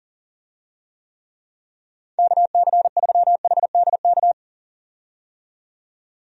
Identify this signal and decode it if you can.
Morse code, heard as a keyed tone, about 40 wpm, 700 Hz. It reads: KC3HDK
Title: USA Call Signs - 6 Characters - 40wpm